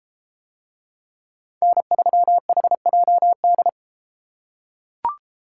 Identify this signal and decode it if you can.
Morse code, heard as a keyed tone, about 33 wpm, 700 Hz. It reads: N3HJD